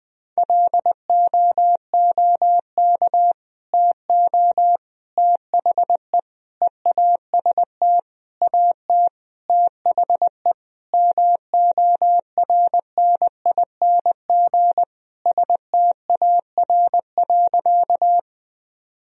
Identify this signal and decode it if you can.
Morse code, heard as a keyed tone, about 21 wpm, 700 Hz. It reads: LOOK TO THE EAST AT THE MORNING STAR.
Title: Sentences from Top 1000 Words - Review Part 3 - 20wpm